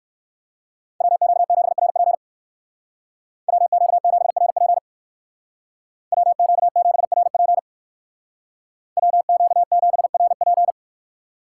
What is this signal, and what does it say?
Morse code, heard as a keyed tone, about 59 wpm, 700 Hz. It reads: WQ7RP WQ7RP WQ7RP WQ7RP